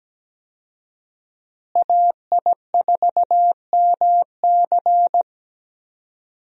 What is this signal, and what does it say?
Morse code, heard as a keyed tone, about 17 wpm, 700 Hz. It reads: AI4MC
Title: USA Call Signs - 5 Characters - Repeated 3x - 17wpm